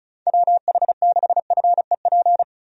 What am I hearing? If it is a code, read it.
Morse code, heard as a keyed tone, about 34 wpm, 700 Hz. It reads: WH6FEP
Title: USA Call Signs - 6 Characters - 35wpm